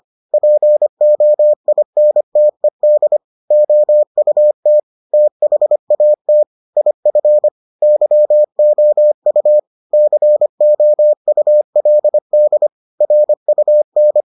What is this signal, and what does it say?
Morse code, heard as a keyed tone, about 25 wpm, 600 Hz. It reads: POINTED OUT THAT IF YOU COULD RUN